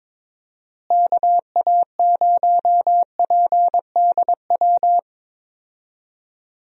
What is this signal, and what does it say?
Morse code, heard as a keyed tone, about 22 wpm, 700 Hz. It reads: KA0PDW
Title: USA Call Signs - 6 Characters - Repeated 3x - 22wpm